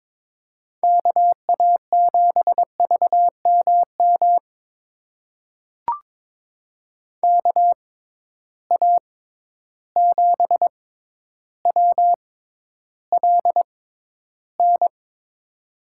Morse code, heard as a keyed tone, about 22 wpm, spaced as for 11 wpm, 700 Hz. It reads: KA7VMM KA7WLN
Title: USA Call Signs - 6x Character Spacing - 22wpm